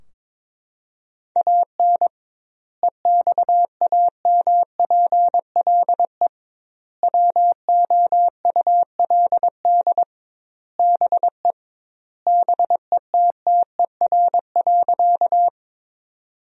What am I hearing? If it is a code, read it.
Morse code, heard as a keyed tone, about 22 wpm, 700 Hz. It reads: AN EXAMPLE WOULD BE BETTER.
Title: 2x Word-Spacing - Sentences from Top 400 Words - 22wpm